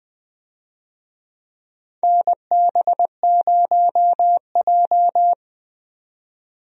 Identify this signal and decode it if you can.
Morse code, heard as a keyed tone, about 20 wpm, 700 Hz. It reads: NB0J
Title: USA Call Signs - 4 Characters - 20wpm